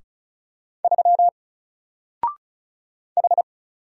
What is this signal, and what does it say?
Morse code, heard as a keyed tone, about 36 wpm, 700 Hz. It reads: 3 H
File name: Single-Letter-Number-35wpm